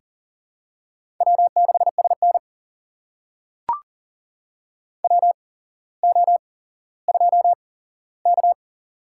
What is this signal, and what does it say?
Morse code, heard as a keyed tone, about 40 wpm, spaced as for 17 wpm, 700 Hz. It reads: W6SN WO2K
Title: USA Call Signs - 4 Characters - 8x Character Spacing - 40wpm